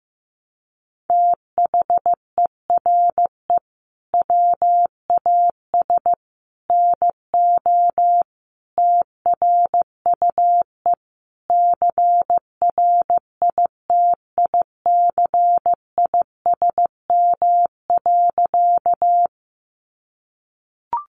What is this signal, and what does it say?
Morse code, heard as a keyed tone, about 15 wpm, 700 Hz. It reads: THERE WAS NO TRUE CRITICISM.
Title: Sentences from Top 1300 Words 15wpm